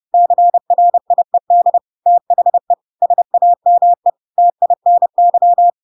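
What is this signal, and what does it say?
Morse code, heard as a keyed tone, about 30 wpm, 700 Hz. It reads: CRIED THE SAME TINY
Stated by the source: Chapter 2 of The Adventures of Pinocchio read as a book in Morse code (CW) at 30 wpm, by ik1hge.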